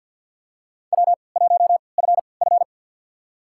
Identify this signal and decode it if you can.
Morse code, heard as a keyed tone, about 53 wpm, spaced as for 29 wpm, 700 Hz. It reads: W1FF
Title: USA Call Signs - 4 Characters - 3x Character Spacing - 50wpm